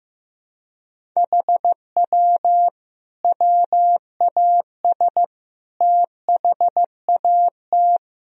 Morse code, heard as a keyed tone, about 16 wpm, 700 Hz. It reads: HW WAS THAT
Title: Sentences from Top 500 Words and Abbreviations 15wpm